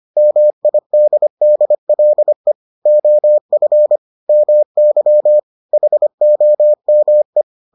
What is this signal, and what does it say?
Morse code, heard as a keyed tone, about 25 wpm, 600 Hz. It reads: MIDDLE OF MY HOME